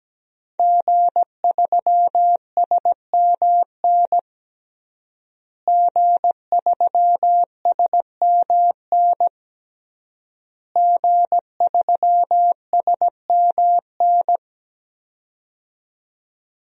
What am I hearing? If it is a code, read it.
Morse code, heard as a keyed tone, about 17 wpm, 700 Hz. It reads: G3SMN G3SMN G3SMN
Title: DXCC-100 Call Signs - 5 Characters - Repeated 3x - 17wpm